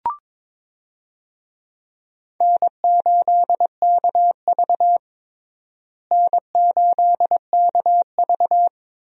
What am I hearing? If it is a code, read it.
Morse code, heard as a keyed tone, about 22 wpm, 700 Hz. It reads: N8KV N8KV
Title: USA Call Signs - 4 Characters - Repeated 3x - 22wpm